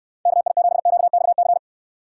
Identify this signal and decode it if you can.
Morse code, heard as a keyed tone, about 69 wpm, 700 Hz. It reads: KI7QXC